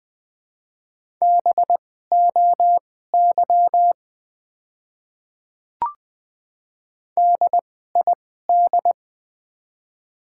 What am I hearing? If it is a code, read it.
Morse code, heard as a keyed tone, about 20 wpm, spaced as for 14 wpm, 700 Hz. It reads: BOY DID
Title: 2x Character Spacing - Top 200 Words - 20wpm